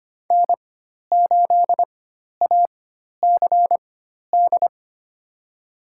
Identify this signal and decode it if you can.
Morse code, heard as a keyed tone, about 25 wpm, spaced as for 12 wpm, 700 Hz. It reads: N8ACD
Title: USA Call Signs - 5 Characters - 4x Character Spacing - 25wpm